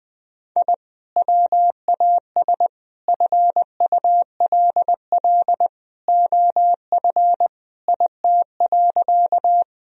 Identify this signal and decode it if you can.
Morse code, heard as a keyed tone, about 20 wpm, 700 Hz. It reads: I WAS FULL OF IT.
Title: Sentences from Top 1000 Words - Review Part 3 - 20wpm